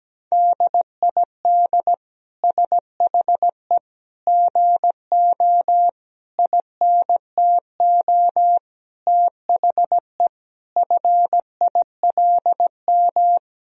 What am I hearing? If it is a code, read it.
Morse code, heard as a keyed tone, about 17 wpm, 700 Hz. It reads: DID SHE GO INTO THE FILM
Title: Sentences from Top 1000 Words - Review Part 3 - 17wpm